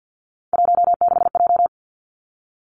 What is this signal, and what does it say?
Morse code, heard as a keyed tone, about 50 wpm, 700 Hz. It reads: 16J
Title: Three Challenging Letter-Number Combinations 50wpm